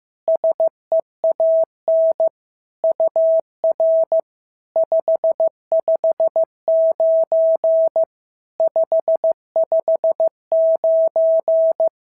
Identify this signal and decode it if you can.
Morse code, heard as a keyed tone, about 15 wpm, 650 Hz. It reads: SEAN UR 559 559
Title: SOTA QSO Practice 15wpm